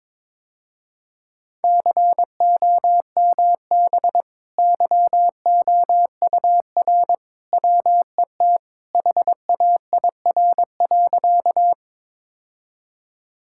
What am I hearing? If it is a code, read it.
Morse code, heard as a keyed tone, about 22 wpm, 700 Hz. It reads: COMB YOUR WET HAIR.
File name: practice / 22wpm / Easy Sentences with Hints - 22wpm